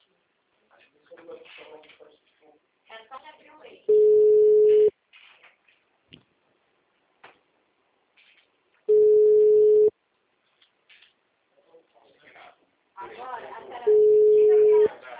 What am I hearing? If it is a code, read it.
Ringback tone (Europe, 425 Hz), with speech.